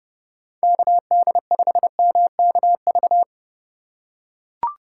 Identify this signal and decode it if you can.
Morse code, heard as a keyed tone, about 30 wpm, 700 Hz. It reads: KD5MKV
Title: USA Call Signs - 6 Characters - 4x Character Spacing - 30wpm